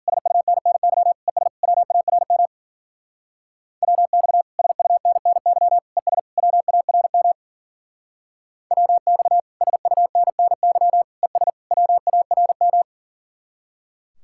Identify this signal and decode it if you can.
Morse code, heard as a keyed tone, about 52 wpm, 700 Hz. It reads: SUNNY ES WARM WX SUNNY ES WARM WX SUNNY ES WARM